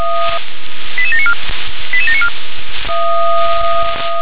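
Five-tone selective call (ZVEI), heard as quick runs of five tones, tones of about 80 ms. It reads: 88794 88794
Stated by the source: Five-tone ZVEI selective call, SDR-received, address 88711.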